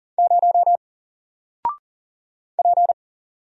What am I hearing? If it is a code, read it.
Morse code, heard as a keyed tone, about 38 wpm, 700 Hz. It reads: 0 P
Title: Single Character (Letter:Number:Punctuation:Prosign) 40wpm